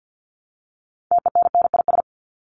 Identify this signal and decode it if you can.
Morse code, heard as a keyed tone, about 50 wpm, 700 Hz. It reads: TENNIS
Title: Top 1501-1750 Words 50wpm